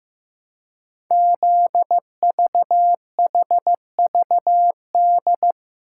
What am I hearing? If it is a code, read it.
Morse code, heard as a keyed tone, about 15 wpm, 700 Hz. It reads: ZVHVD
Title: Code Groups - 5 Letters - 15wpm